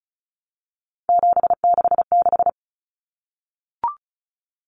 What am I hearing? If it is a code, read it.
Morse code, heard as a keyed tone, about 34 wpm, 700 Hz. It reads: 766
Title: Common Cut Numbers - 3-Digit 35wpm